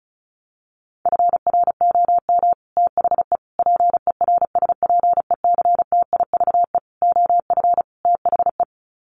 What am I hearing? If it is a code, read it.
Morse code, heard as a keyed tone, about 34 wpm, 700 Hz. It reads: FROM THE PERSPECTIVE OF THE
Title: Sets of 5 Words Encore2 35wpm